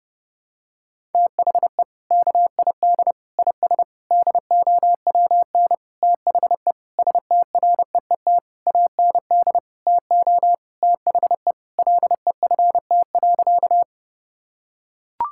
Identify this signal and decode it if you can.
Morse code, heard as a keyed tone, about 30 wpm, 700 Hz. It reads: THE KID IS DOWN THE STREET AND TO THE LEFT.
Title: Sentences from Top 1600 Words 30wpm